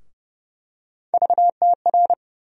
Morse code, heard as a keyed tone, about 32 wpm, 700 Hz. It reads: VTR
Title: USA Call Sign Postfixes - 3 Letters 30wpm